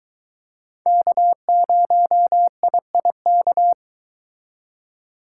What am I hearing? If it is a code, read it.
Morse code, heard as a keyed tone, about 22 wpm, 700 Hz. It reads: K0IIK